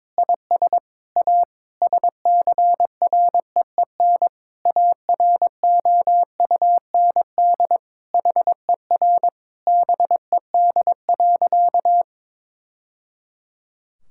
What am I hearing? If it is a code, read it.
Morse code, heard as a keyed tone, about 22 wpm, 700 Hz. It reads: IS A SCREEN AROUND HER BED.
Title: Sentences from Top 1000 Words - Review Part 6 - 22wpm